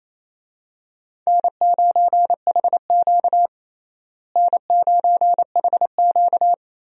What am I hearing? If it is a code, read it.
Morse code, heard as a keyed tone, about 28 wpm, 700 Hz. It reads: N9HQ N9HQ
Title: USA Call Signs - 4 Characters - Repeated 3x - 28wpm